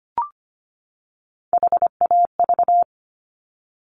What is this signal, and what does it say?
Morse code, heard as a keyed tone, about 25 wpm, 700 Hz. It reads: HAV
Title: Top Three Letter Combinations in Words 25wpm